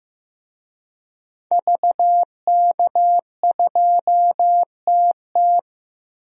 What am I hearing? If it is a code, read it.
Morse code, heard as a keyed tone, about 15 wpm, 700 Hz. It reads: VK2TT
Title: DXCC-100 Call Signs - 5 Characters - Repeated 3x - 15wpm